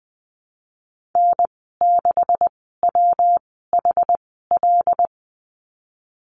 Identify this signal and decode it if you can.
Morse code, heard as a keyed tone, about 20 wpm, spaced as for 15 wpm, 700 Hz. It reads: N6WHL
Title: 128 Lesson 21g - Number 6 - Call Signs 20wpm